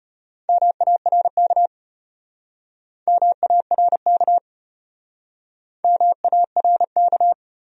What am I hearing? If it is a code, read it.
Morse code, heard as a keyed tone, about 36 wpm, 700 Hz. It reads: MARK MARK MARK